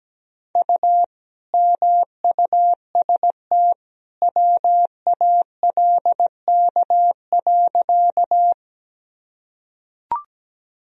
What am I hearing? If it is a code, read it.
Morse code, heard as a keyed tone, about 17 wpm, 700 Hz. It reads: U MUST WALK.
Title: Sentences from Top 300 Words and Abbreviations 17wpm